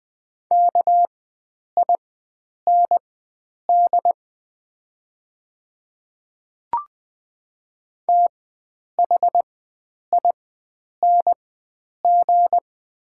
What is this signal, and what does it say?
Morse code, heard as a keyed tone, about 20 wpm, spaced as for 9 wpm, 700 Hz. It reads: KIND THING
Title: 4x Character Spacing - Top 500 Words - 20wpm